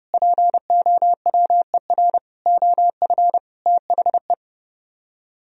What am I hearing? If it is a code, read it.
Morse code, heard as a keyed tone, about 30 wpm, 700 Hz. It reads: POWER OF THE